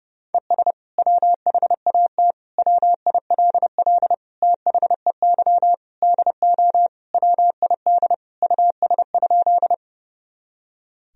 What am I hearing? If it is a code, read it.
Morse code, heard as a keyed tone, about 30 wpm, 700 Hz. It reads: ES WHAT WILL THEY DO WID US?
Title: Sentences from Top 200 Words and Abbreviations 30wpm